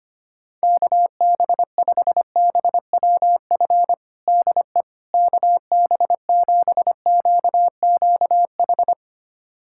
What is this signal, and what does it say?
Morse code, heard as a keyed tone, about 25 wpm, 700 Hz. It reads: KB5BWF DE KB7QQH
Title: USA Callsign DE Callsign 25wpm